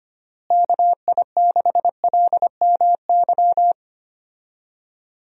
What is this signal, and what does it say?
Morse code, heard as a keyed tone, about 25 wpm, 700 Hz. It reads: KI6LMY
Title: USA Call Signs - 6 Characters - Repeated 3x - 25wpm